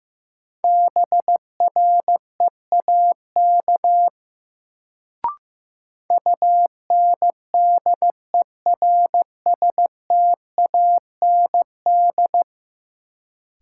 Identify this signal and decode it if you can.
Morse code, heard as a keyed tone, about 15 wpm, 700 Hz. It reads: BREAK UNDERSTAND
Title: Sets of 4 Words - New Word Drill - 15wpm